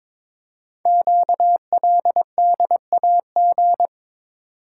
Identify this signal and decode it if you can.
Morse code, heard as a keyed tone, about 22 wpm, 700 Hz. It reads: QLDAG
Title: Code Groups - 5 Letters - 22wpm